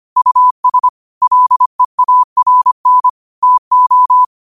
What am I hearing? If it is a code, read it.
Morse code, heard as a keyed tone, about 25 wpm, 1000 Hz. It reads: US LEARN TO